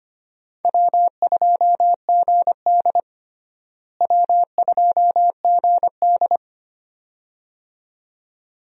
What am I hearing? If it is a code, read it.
Morse code, heard as a keyed tone, about 25 wpm, 700 Hz. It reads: W2GD W2GD
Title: USA Call Signs - 4 Characters - Repeated 2x - 25wpm